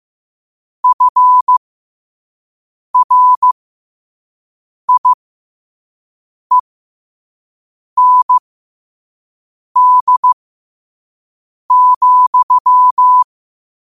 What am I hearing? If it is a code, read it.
Morse code, heard as a keyed tone, about 15 wpm, spaced as for 5 wpm, 1000 Hz. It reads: FRIEND,
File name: qotdfiles / 5wpm